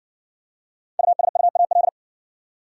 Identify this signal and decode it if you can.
Morse code, heard as a keyed tone, about 60 wpm, 700 Hz. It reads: USUAL